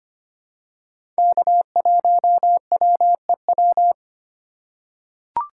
Morse code, heard as a keyed tone, about 25 wpm, 700 Hz. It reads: K1WEW
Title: USA Call Signs - 4x Character Spacing - 25wpm